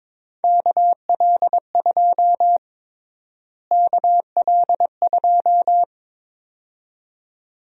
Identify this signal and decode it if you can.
Morse code, heard as a keyed tone, about 22 wpm, 700 Hz. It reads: KL2 KL2
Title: USA Call Sign Prefixes - Repeated 2x - 22wpm